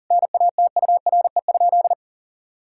Morse code, heard as a keyed tone, about 37 wpm, 700 Hz. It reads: NATURE?
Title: Sentences from Top 1000 Words - Review Part 5 - 40wpm